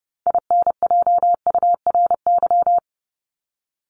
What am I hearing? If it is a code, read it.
Morse code, heard as a keyed tone, about 30 wpm, 700 Hz. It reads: INJURY
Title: Top 1751-2000 Words 30wpm